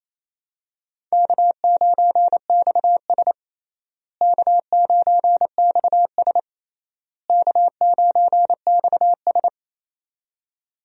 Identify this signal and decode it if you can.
Morse code, heard as a keyed tone, about 28 wpm, 700 Hz. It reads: K9XS K9XS K9XS
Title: USA Call Signs - 4 Characters - Repeated 3x - 28wpm